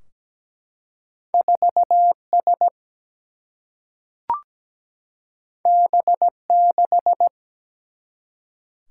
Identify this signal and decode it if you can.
Morse code, heard as a keyed tone, about 17 wpm, 700 Hz. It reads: 4 S B 6
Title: Single-Letter Single-Number 17wpm